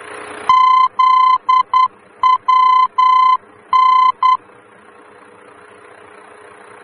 Morse code, heard as a keyed tone, about 10 wpm, 1020 Hz. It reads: ZWN
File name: germany / mx / ZWN-D-VOR